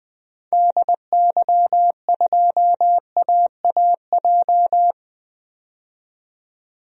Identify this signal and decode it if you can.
Morse code, heard as a keyed tone, about 20 wpm, 700 Hz. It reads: DY2AAJ
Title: DXCC-100 Call Signs - 6 Characters - 20wpm